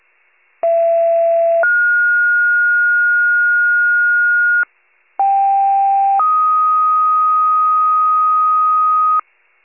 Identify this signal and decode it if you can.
Two-tone page, heard as a one-second tone, then a three-second tone, about 670 then 1430 Hz, about 790 then 1250 Hz, with noise. It is heard from 0.6 to 9.2 s.